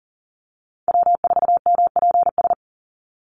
Morse code, heard as a keyed tone, about 38 wpm, 700 Hz. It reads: W4MPS
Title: NA SOTA Chaser Callsigns 40wpm